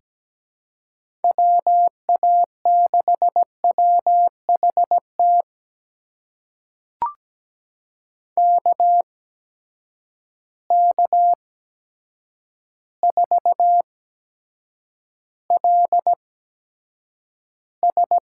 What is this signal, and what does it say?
Morse code, heard as a keyed tone, about 17 wpm, spaced as for 8 wpm, 700 Hz. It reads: WA6WHT KK4LS
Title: USA Call Signs - 8x Character Spacing - 17wpm